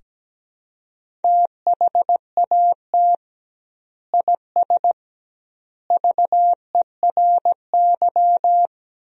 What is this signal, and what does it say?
Morse code, heard as a keyed tone, about 17 wpm, 700 Hz. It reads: THAT IS VERY
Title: 2x Word-Spacing - Sentences from Top 400 Words - 17wpm